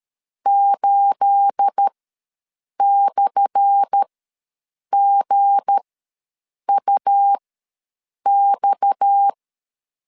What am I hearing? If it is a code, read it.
Morse code, heard as a keyed tone, about 13 wpm, spaced as for 7 wpm, 800 Hz. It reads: 8/GUX
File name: LEZIONI / 10 SIGNS / 10A SLASH ws 12wpm 3443